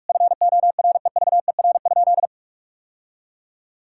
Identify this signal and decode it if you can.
Morse code, heard as a keyed tone, about 46 wpm, 700 Hz. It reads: FOREVER?
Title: Sentences from Top 1600 Words 45wpm